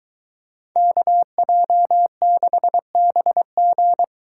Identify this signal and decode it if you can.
Morse code, heard as a keyed tone, about 22 wpm, 700 Hz. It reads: KJ6BG